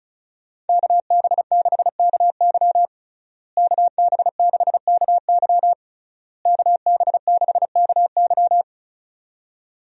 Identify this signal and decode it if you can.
Morse code, heard as a keyed tone, about 36 wpm, 700 Hz. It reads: KB6KY KB6KY KB6KY
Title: USA Call Signs - 5 Characters - Repeated 3x - 35wpm